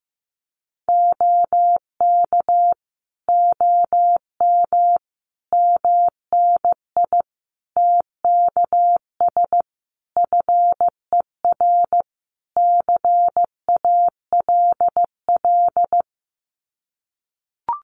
Morse code, heard as a keyed tone, about 15 wpm, 700 Hz. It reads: OK OM MNI TKS FER CALL
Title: Common QSO Phrases 15wpm